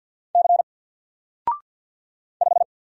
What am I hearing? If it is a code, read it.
Morse code, heard as a keyed tone, about 50 wpm, 700 Hz. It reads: C 5
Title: Single Character (Letter:Number:Punctuation:Prosign) 50wpm